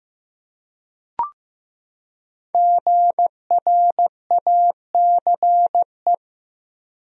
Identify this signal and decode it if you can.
Morse code, heard as a keyed tone, about 15 wpm, 700 Hz. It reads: GRACE
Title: USA Common Names - Max 5 Letters - 15wpm